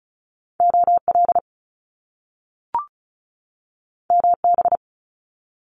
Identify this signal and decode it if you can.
Morse code, heard as a keyed tone, about 36 wpm, 700 Hz. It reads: OL MB
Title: Top Two Letter Combinations in Words 35wpm